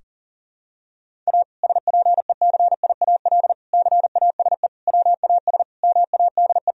Morse code, heard as a keyed tone, about 40 wpm, 700 Hz. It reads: A SPECIAL CASE WAS MADE
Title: Sentences from Top 1000 Words - Review Part 3 - 40wpm